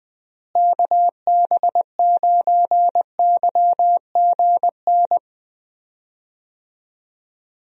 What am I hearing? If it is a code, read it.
Morse code, heard as a keyed tone, about 20 wpm, 700 Hz. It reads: KB9YGN
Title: USA Call Signs - 6 Characters - Repeated 3x - 20wpm